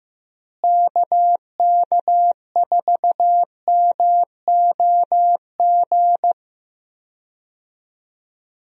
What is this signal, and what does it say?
Morse code, heard as a keyed tone, about 16 wpm, 700 Hz. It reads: KK4MOG
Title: USA Call Signs - 6 Characters - Repeated 3x - 15wpm